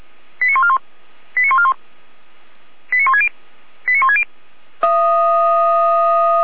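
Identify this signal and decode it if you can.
Five-tone selective call (ZVEI), heard as quick runs of five tones, tones of about 70 ms. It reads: 89141 89141 89160 89160
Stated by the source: ZVEI five-tone selective call received by SDR.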